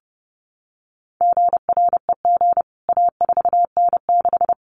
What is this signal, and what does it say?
Morse code, heard as a keyed tone, about 30 wpm, 700 Hz. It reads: GREG A4N6
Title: CWT Exchange 30wpm